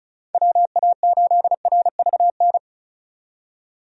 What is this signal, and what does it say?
Morse code, heard as a keyed tone, about 36 wpm, 700 Hz. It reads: WA8RVN
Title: USA Call Signs - 6 Characters - 35wpm